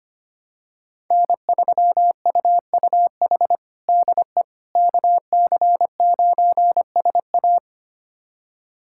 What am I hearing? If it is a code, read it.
Morse code, heard as a keyed tone, about 25 wpm, 700 Hz. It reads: N3UUH DE KC9SA
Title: USA Callsign DE Callsign - 5 Characters - 25wpm